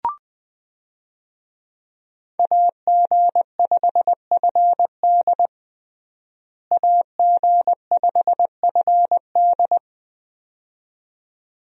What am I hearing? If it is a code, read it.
Morse code, heard as a keyed tone, about 21 wpm, 700 Hz. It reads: AG5FD AG5FD
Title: USA Call Signs - 5 Characters - Repeated 2x - 20wpm